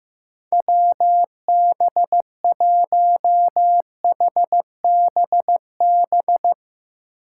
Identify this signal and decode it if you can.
Morse code, heard as a keyed tone, about 16 wpm, 700 Hz. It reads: WB1HBB
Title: Speed Racing - USA 2x3 Call Signs 15wpm